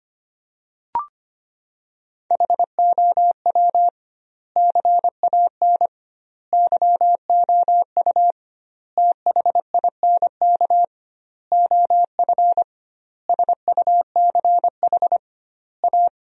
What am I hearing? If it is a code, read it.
Morse code, heard as a keyed tone, about 25 wpm, 700 Hz. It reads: HOW CAN YOU THINK OF SUCH A
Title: 2x Word-Spacing - Sentences from Top 200 Words - 25wpm